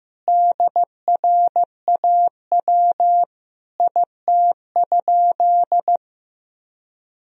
Morse code, heard as a keyed tone, about 16 wpm, 700 Hz. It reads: DRAW IT?
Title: Sentences from Top 1000 Words - Review Part 1 - 15wpm